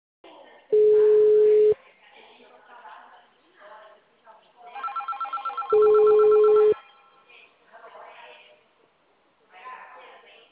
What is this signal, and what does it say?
Ringback tone (Europe, 425 Hz), with speech.